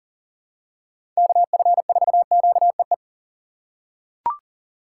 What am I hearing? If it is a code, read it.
Morse code, heard as a keyed tone, about 39 wpm, 700 Hz. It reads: KF4QEE
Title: USA Call Signs - 6 Characters - 3x Character Spacing - 40wpm